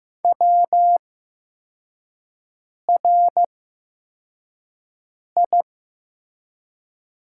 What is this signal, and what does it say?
Morse code, heard as a keyed tone, about 15 wpm, 700 Hz. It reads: W R I
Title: 8x Character Spacing - Top 100 Words - 15wpm